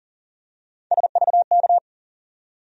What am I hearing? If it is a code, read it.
Morse code, heard as a keyed tone, about 40 wpm, 700 Hz. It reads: SVK
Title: USA Call Sign Postfixes - 3 Letters 40wpm